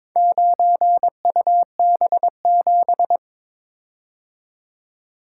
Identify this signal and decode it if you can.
Morse code, heard as a keyed tone, about 22 wpm, 700 Hz. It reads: 9UB7
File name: practice / 22wpm / Four Letter-Number Combinations - 22wpm